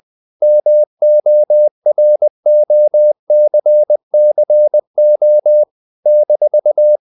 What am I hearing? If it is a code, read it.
Morse code, heard as a keyed tone, about 20 wpm, 600 Hz. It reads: MOROCCO -
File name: CW / Practice / NationalCapitals20